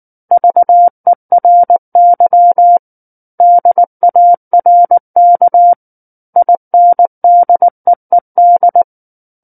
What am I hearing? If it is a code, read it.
Morse code, heard as a keyed tone, about 19 wpm, 700 Hz. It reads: VERY DARK INDEED
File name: Book One / 10-Chapter 10 - 19 WPM